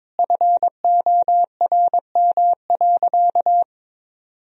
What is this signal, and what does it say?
Morse code, heard as a keyed tone, about 22 wpm, 700 Hz. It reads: FORM.
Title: Sentences from Top 1000 Words - Review Part 1 - 22wpm